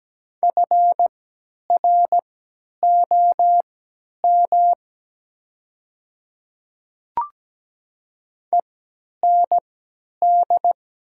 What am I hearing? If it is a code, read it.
Morse code, heard as a keyed tone, about 17 wpm, spaced as for 10 wpm, 700 Hz. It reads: FROM END
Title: 3x Character Spacing - Top 200 Words - 17wpm